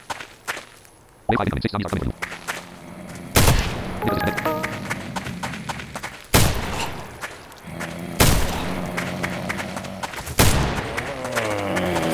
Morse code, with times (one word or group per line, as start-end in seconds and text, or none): none